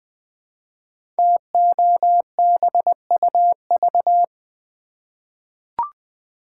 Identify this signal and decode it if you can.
Morse code, heard as a keyed tone, about 20 wpm, 700 Hz. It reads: TOBUV
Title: Code Groups - 5 Letters - 20wpm